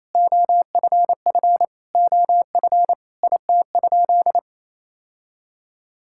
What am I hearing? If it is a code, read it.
Morse code, heard as a keyed tone, about 28 wpm, 700 Hz. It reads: OFF OF IT?
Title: Sentences from Top 1000 Words - Review Part 5 - 28wpm